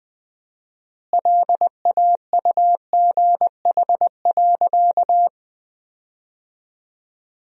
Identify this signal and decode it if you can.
Morse code, heard as a keyed tone, about 20 wpm, 700 Hz. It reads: LAUGH.
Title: 4x Word-Spacing - Sentences from Top 500 Words - 20wpm